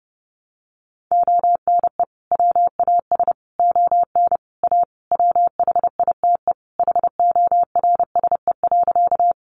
Morse code, heard as a keyed tone, about 30 wpm, 700 Hz. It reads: ONE WAS ON A WHITE HORSE.
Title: Sentences from Top 300 Words 30wpm